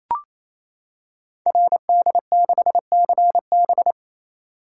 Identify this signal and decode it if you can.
Morse code, heard as a keyed tone, about 27 wpm, 700 Hz. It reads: RD6CB
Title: Code Groups - 5 Letter:Numbers - 28wpm